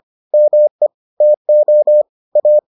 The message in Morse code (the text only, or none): ME TO A